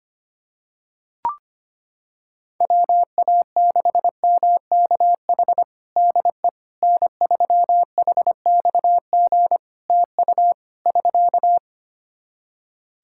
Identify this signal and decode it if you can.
Morse code, heard as a keyed tone, about 25 wpm, 700 Hz. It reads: WA6MKH DE N3HXG TU <SK>
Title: Practice QSO - Exchange 4 - Ending - 25wpm